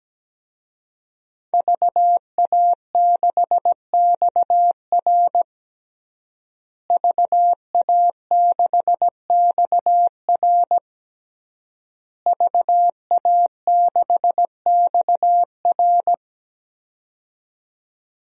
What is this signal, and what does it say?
Morse code, heard as a keyed tone, about 17 wpm, 700 Hz. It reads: VA6XR VA6XR VA6XR
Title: DXCC-100 Call Signs - 5 Characters - Repeated 3x - 17wpm